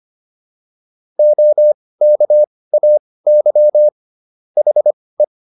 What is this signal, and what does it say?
Morse code, heard as a keyed tone, about 25 wpm, spaced as for 18 wpm, 600 Hz. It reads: OKAY HE
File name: SS108 18